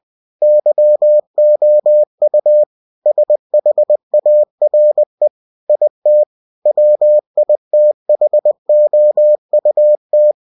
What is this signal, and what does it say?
Morse code, heard as a keyed tone, about 20 wpm, 600 Hz. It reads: YOU SHARE IT WITHOUT
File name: cw-books / winnie-the-pooh / winnie-20wpm-page-0114